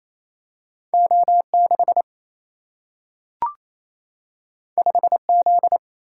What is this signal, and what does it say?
Morse code, heard as a keyed tone, about 28 wpm, 700 Hz. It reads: O 6 5 Z
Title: Single-Letter Single-Number 28wpm